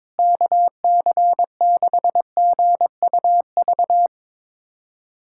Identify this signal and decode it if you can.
Morse code, heard as a keyed tone, about 22 wpm, 700 Hz. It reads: KC6GUV
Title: USA Callsign DE Callsign 22wpm